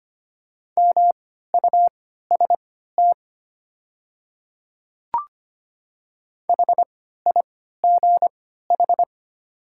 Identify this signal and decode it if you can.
Morse code, heard as a keyed tone, about 25 wpm, spaced as for 14 wpm, 700 Hz. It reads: MUST HIGH
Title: 3x Character Spacing - Top 200 Words - 25wpm